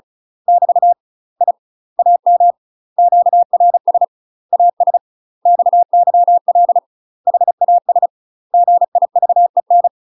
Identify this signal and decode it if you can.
Morse code, heard as a keyed tone, about 34 wpm, 700 Hz. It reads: <BT> I AM QRS AS XYL HAS GIVEN